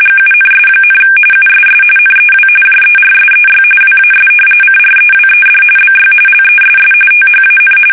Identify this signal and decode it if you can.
RTTY, about 45.45 baud. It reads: GUNS, USS NEW JERSEY WAS CREWED BY LESS THAN
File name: fsk850 / battleshipnj